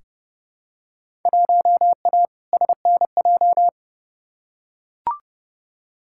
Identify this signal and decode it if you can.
Morse code, heard as a keyed tone, about 30 wpm, 700 Hz. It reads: 1A SNJ
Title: Field Day Exchanges - 30wpm